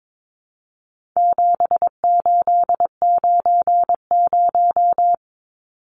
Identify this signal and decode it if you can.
Morse code, heard as a keyed tone, about 22 wpm, 700 Hz. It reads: 7890
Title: Numbers - 4-Digit 22wpm